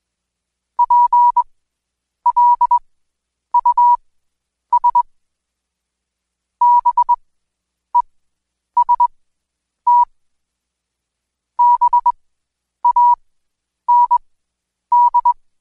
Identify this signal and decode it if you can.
Morse code, heard as a keyed tone, about 20 wpm, spaced as for 9 wpm, 970 Hz. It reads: PLUS BEST BAND
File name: Code Quick / AudioDownld2 / Seg115-Day20-DayDrill 20